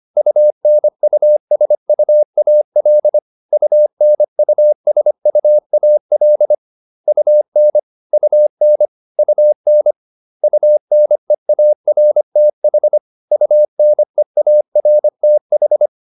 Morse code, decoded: UNUSUAL UNUSUAL UN UN UN UNEARTH UNEARTH

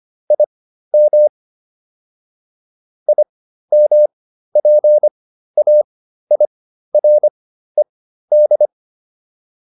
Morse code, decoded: IM IMPAIRED

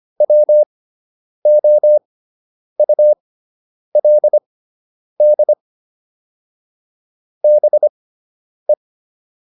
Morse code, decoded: WOULD BE

